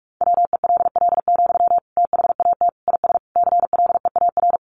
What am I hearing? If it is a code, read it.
Morse code, heard as a keyed tone, about 44 wpm, 700 Hz. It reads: WELL, THAT IS CLEAR
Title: Sentences from Top 900 Words 45wpm